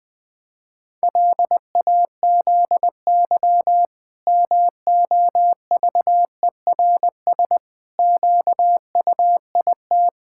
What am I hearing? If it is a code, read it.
Morse code, heard as a keyed tone, about 20 wpm, 700 Hz. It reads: LAZY MOVERS QUIT
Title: Pangram Challenge v2 - 20wpm